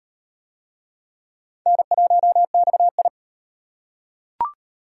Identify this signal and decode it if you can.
Morse code, heard as a keyed tone, about 38 wpm, 700 Hz. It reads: N1XI